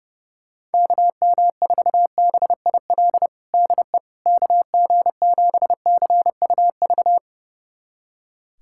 Morse code, heard as a keyed tone, about 30 wpm, 700 Hz. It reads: KM4BIL DE KG7CUV
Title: USA Callsign DE Callsign - 6 Characters - 30wpm